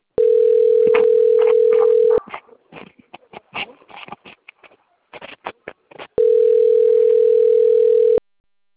Ringback tone (North America), 440 + 480 Hz.
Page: Sao Roque